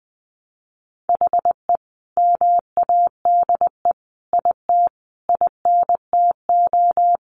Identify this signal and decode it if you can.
Morse code, heard as a keyed tone, about 19 wpm, 700 Hz. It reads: HE MADE IT INTO
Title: Sentences from Top 1300 Words 20wpm